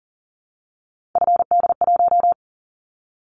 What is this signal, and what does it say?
Morse code, heard as a keyed tone, about 42 wpm, 700 Hz. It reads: FD1
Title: Callsign Prefixes for Top 50 DXCC - 40wpm